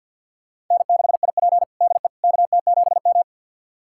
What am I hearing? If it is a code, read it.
Morse code, heard as a keyed tone, about 51 wpm, 700 Hz. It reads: N6IP DE KT7M
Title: USA Callsign DE Callsign - 4 Characters - 50wpm